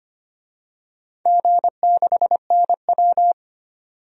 Morse code, heard as a keyed tone, about 25 wpm, 700 Hz. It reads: G6NW
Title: DXCC-100 Call Signs - 4 Characters - 25wpm